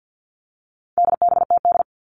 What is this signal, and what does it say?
Morse code, heard as a keyed tone, about 50 wpm, 700 Hz. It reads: DBTD
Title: All Cut Numbers - 4-Digit 50wpm